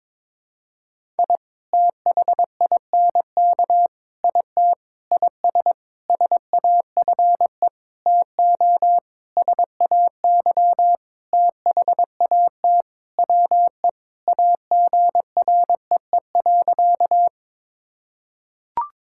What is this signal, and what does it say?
Morse code, heard as a keyed tone, about 21 wpm, 700 Hz. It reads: I THINK IT IS SAFE TO SAY THAT WE AGREE.
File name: practice / 22wpm / Sentences with 8-Word Phrases - 22wpm